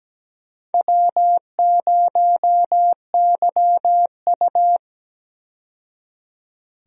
Morse code, heard as a keyed tone, about 17 wpm, 700 Hz. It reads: W0YU
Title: USA Call Signs - 4 Characters - 17wpm